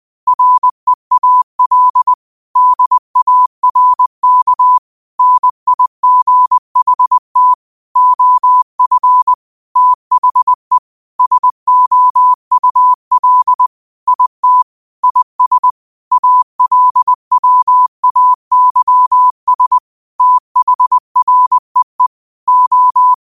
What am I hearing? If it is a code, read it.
Morse code, heard as a keyed tone, about 20 wpm, 1000 Hz. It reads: REAL DARK NIGHT OF THE SOUL IT IS ALWAYS THREE O
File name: qotdfiles / 20wpm